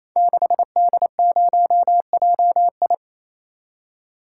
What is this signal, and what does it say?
Morse code, heard as a keyed tone, about 27 wpm, 700 Hz. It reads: 6D0JI
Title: Code Groups - 5 Letter:Numbers - 28wpm